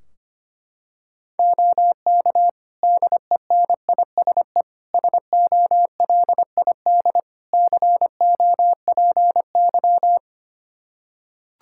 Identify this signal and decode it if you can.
Morse code, heard as a keyed tone, about 25 wpm, 700 Hz. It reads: OK DENISE SOLID COPY
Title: Practice QSO - Exchange 3 - More Info - 25wpm